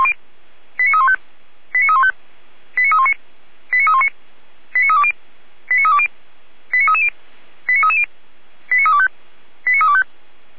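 Five-tone selective call (ZVEI), heard as quick runs of five tones, tones of about 70 ms. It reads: 89315 89315 89319 89319 89320 89320 89330 89330 89325 89325
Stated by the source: ZVEI five-tone selective call received by SDR, archived by the Feuerwehr Orlamünde fire brigade.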